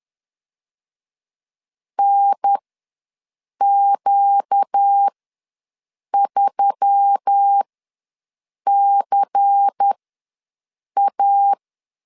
Morse code, heard as a keyed tone, about 10 wpm, spaced as for 6 wpm, 800 Hz. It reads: NQ3CA
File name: LEZIONI / 5 LBVC / 5 LBVC ws 10wpm 3733